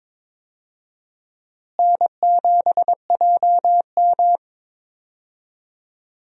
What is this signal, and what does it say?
Morse code, heard as a keyed tone, about 22 wpm, 700 Hz. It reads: N7JM
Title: USA Call Signs - 4 Characters - Repeated 3x - 22wpm